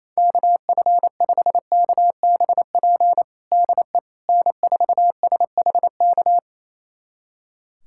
Morse code, heard as a keyed tone, about 28 wpm, 700 Hz. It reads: KF5KBP DE N4SHK
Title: USA Callsign DE Callsign 28wpm